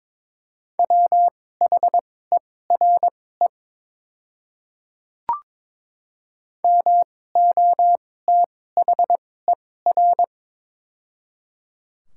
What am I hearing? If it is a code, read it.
Morse code, heard as a keyed tone, about 21 wpm, spaced as for 16 wpm, 700 Hz. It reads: WHERE MOTHER
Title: 2x Character Spacing - Top 200 Words - 22wpm